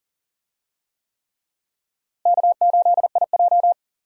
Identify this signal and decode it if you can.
Morse code, heard as a keyed tone, about 40 wpm, 700 Hz. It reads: K8IJ
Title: USA Call Signs - 4 Characters - Repeated 3x - 40wpm